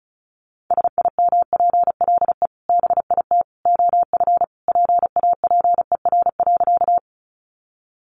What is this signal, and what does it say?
Morse code, heard as a keyed tone, about 34 wpm, 700 Hz. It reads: SIMPLE BIT OF PAPER.
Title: Sentences from Top 900 Words 35wpm